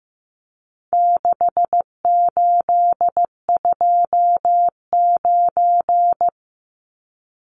Morse code, heard as a keyed tone, about 15 wpm, 700 Hz. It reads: 6829
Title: Numbers - 4-Digit 15wpm